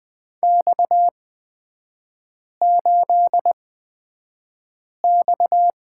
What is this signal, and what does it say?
Morse code, heard as a keyed tone, about 20 wpm, spaced as for 5 wpm, 690 Hz. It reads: X8X
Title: 229 Lesson 37e - Letter x - Three Characters 20wpm